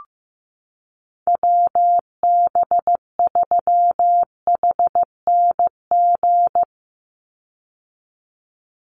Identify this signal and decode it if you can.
Morse code, heard as a keyed tone, about 15 wpm, 700 Hz. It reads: WB3HNG
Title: USA Call Signs 15wpm